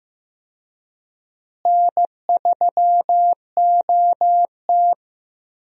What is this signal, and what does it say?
Morse code, heard as a keyed tone, about 15 wpm, 700 Hz. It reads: N3OT
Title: USA Call Signs - 4 Characters - Repeated 3x - 15wpm